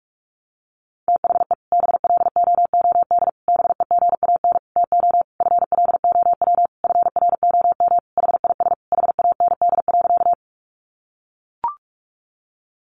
Morse code, heard as a keyed tone, about 44 wpm, 700 Hz. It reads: THE BLOOD BEGAN TO FLOW FROM HIS HAND.
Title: Sentences from Top 1200 Words 45wpm